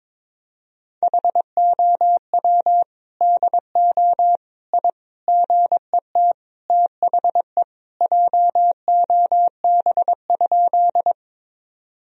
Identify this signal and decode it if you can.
Morse code, heard as a keyed tone, about 22 wpm, 700 Hz. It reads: HOW DO I GET THE JOB?
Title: Sentences from Top 1000 Words - Review Part 3 - 22wpm